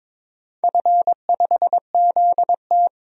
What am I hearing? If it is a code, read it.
Morse code, heard as a keyed tone, about 22 wpm, 700 Hz. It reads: F5ZT
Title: DXCC-100 Call Signs - 4 Characters - 22wpm